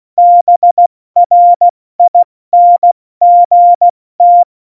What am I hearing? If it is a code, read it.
Morse code, heard as a keyed tone, about 16 wpm, 700 Hz. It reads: BRINGT